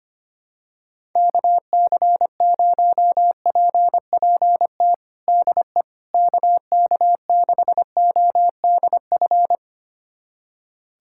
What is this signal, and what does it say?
Morse code, heard as a keyed tone, about 25 wpm, 700 Hz. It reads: KC0PPT DE KK6ODF
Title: USA Callsign DE Callsign 25wpm